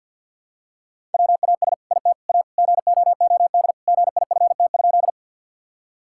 Morse code, heard as a keyed tone, about 50 wpm, 700 Hz. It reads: WAS IT A GOOD GIFT?
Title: Sentences from Top 1000 Words - Review Part 6 - 50wpm